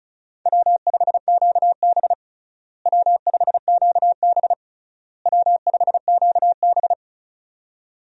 Morse code, heard as a keyed tone, about 36 wpm, 700 Hz. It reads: W5QB W5QB W5QB
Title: USA Call Signs - 4 Characters - Repeated 3x - 35wpm